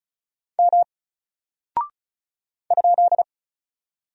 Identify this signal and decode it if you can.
Morse code, heard as a keyed tone, about 36 wpm, 700 Hz. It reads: M ?
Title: Single Character (Letter:Number:Punctuation:Prosign) 35wpm